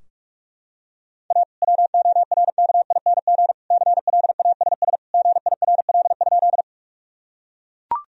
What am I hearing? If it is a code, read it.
Morse code, heard as a keyed tone, about 43 wpm, 700 Hz. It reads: A WORKING CLASS GIRL?
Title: Sentences from Top 1000 Words - Review Part 4 - 45wpm